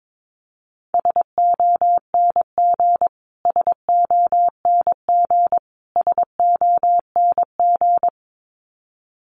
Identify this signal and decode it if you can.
Morse code, heard as a keyed tone, about 22 wpm, 700 Hz. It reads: SONG SONG SONG
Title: Top 500 Words Repeated 22wpm